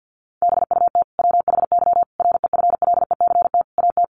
Morse code, heard as a keyed tone, about 50 wpm, 700 Hz. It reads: BUT WHY REFLECT AT
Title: Sentences from Top 1500 Words 50wpm